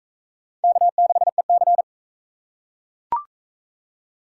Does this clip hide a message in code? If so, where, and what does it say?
Yes: Morse code from 0.6 to 1.8 s, heard as a keyed tone, about 40 wpm, 700 Hz. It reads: K6EC